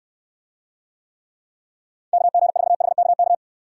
Morse code, heard as a keyed tone, about 69 wpm, 700 Hz. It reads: KG4HKK